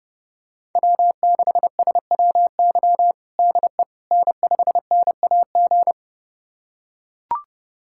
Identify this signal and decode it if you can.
Morse code, heard as a keyed tone, about 30 wpm, 700 Hz. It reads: W6SWY DE N5NAG
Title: USA Callsign DE Callsign 30wpm